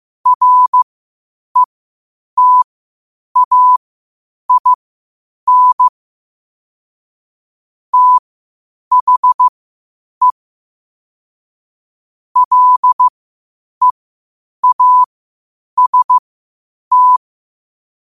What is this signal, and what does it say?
Morse code, heard as a keyed tone, about 15 wpm, spaced as for 8 wpm, 1000 Hz. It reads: RETAIN THE LEAST
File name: qotdfiles / 8wpm